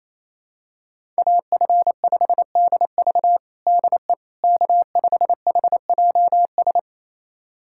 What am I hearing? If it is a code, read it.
Morse code, heard as a keyed tone, about 28 wpm, 700 Hz. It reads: AF5DV DE K5HJS
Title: USA Callsign DE Callsign - 5 Characters - 28wpm